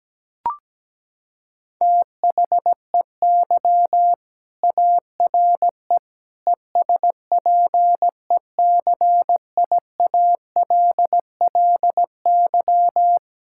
Morse code, heard as a keyed tone, about 17 wpm, 700 Hz. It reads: THEY ARE ESPECIALLY
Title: Sentences from Top 1000 Words - Review Part 5 - 17wpm